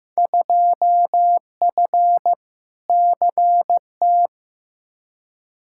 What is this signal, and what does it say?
Morse code, heard as a keyed tone, about 15 wpm, 700 Hz. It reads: 2F CT
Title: Field Day Exchanges Encore - 15wpm